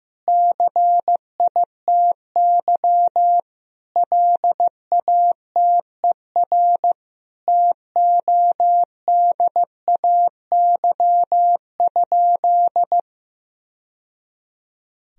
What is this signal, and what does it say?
Morse code, heard as a keyed tone, about 16 wpm, 700 Hz. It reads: CITY LATER TODAY?
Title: Sentences from Top 1000 Words - Review Part 2 - 15wpm